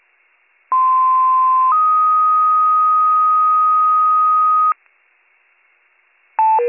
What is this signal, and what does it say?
Two-tone page, heard as a one-second tone, then a three-second tone, about 1040 then 1250 Hz. It is heard from 0.7 to 4.7 s.